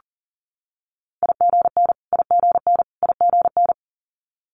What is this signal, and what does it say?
Morse code, heard as a keyed tone, about 40 wpm, 700 Hz. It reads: IGN IGN IGN
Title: Multi-letter Phonemes Repeated 3x 40wpm